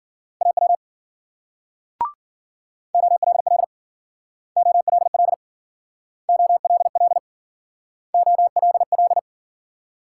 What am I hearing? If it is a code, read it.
Morse code, heard as a keyed tone, about 48 wpm, 700 Hz. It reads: AW OLL OLL OLL OLL
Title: Speed-Racing Multi-letter Phonemes - 40wpm